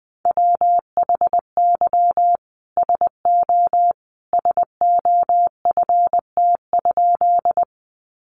Morse code, heard as a keyed tone, about 20 wpm, 700 Hz. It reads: WHY SO SOFT?
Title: Sentences from Top 1500 Words 20wpm